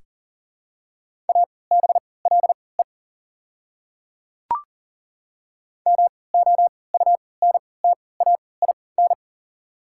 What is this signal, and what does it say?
Morse code, heard as a keyed tone, about 40 wpm, spaced as for 23 wpm, 690 Hz. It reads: ABLE MOUNTAIN
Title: 3x Character Spacing - Top 500 Words - 40wpm